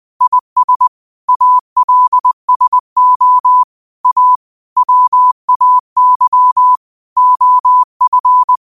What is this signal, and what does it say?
Morse code, heard as a keyed tone, about 20 wpm, 1000 Hz. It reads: IS ALSO A WAY OF
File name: qotdfiles / 20wpm